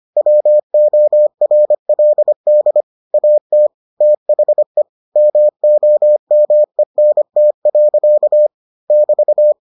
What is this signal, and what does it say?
Morse code, heard as a keyed tone, about 25 wpm, 600 Hz. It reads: WORLD AT THE MOMENT. <BT>